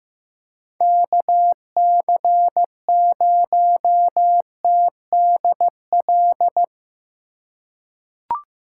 Morse code, heard as a keyed tone, about 15 wpm, 700 Hz. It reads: KC0TDL
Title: USA Call Signs - 6 Characters - 3x Character Spacing - 15wpm